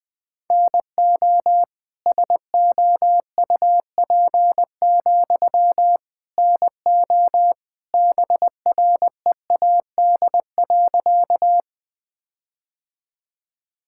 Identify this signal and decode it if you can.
Morse code, heard as a keyed tone, about 21 wpm, 700 Hz. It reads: NO SOUP, NO BREAD.
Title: Sentences from Top 1600 Words 20wpm